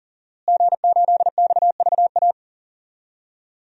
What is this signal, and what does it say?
Morse code, heard as a keyed tone, about 40 wpm, 700 Hz. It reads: G8XVA
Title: DXCC-100 Call Signs - 5 Characters - Repeated 2x - 40wpm